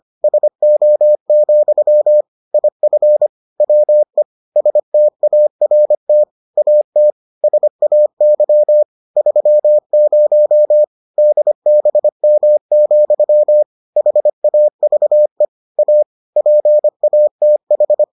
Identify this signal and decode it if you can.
Morse code, heard as a keyed tone, about 25 wpm, 600 Hz. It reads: SO, IF WE START AT SAY 30 DBM, HAVE A PATH